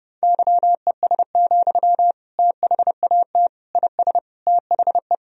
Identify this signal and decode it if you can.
Morse code, heard as a keyed tone, about 32 wpm, 700 Hz. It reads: YES, THAT IS THE
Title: Sentences from Top 1000 Words - Review Part 4 - 30wpm